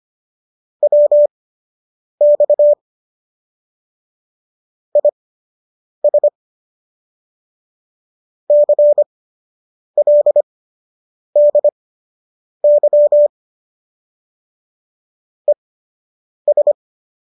Morse code, heard as a keyed tone, about 25 wpm, spaced as for 8 wpm, 590 Hz. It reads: WX IS CLDY ES